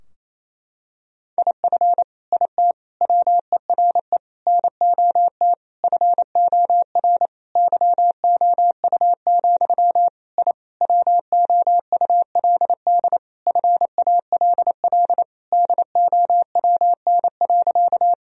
Morse code, heard as a keyed tone, about 28 wpm, 700 Hz. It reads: IF IT WERE NOT FOR YOU, I WOULD FALL DOWN.
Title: Sentences from Top 1000 Words - Review Part 2 - 28wpm